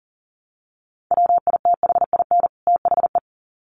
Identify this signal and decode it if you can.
Morse code, heard as a keyed tone, about 40 wpm, 700 Hz. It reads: WITHIN THE
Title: Sets of 2 Words 40wpm